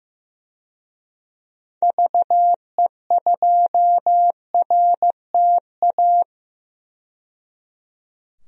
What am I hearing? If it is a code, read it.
Morse code, heard as a keyed tone, about 15 wpm, 700 Hz. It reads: VE2RTA
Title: DXCC-100 Call Signs - 6 Characters - 15wpm